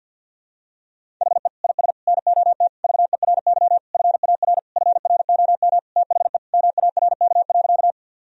Morse code, heard as a keyed tone, about 51 wpm, 700 Hz. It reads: HE IS NOT VERY FAR FROM THE MARK.
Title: Sentences from Top 1000 Words - Review Part 1 - 50wpm